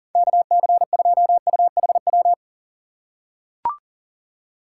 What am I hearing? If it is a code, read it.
Morse code, heard as a keyed tone, about 40 wpm, 700 Hz. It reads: KC2UHW
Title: USA Call Signs - 6 Characters - 4x Character Spacing - 40wpm